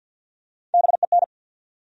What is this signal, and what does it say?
Morse code, heard as a keyed tone, about 50 wpm, 700 Hz. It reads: BEN
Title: USA Common Names - Max 3 Letters - 50wpm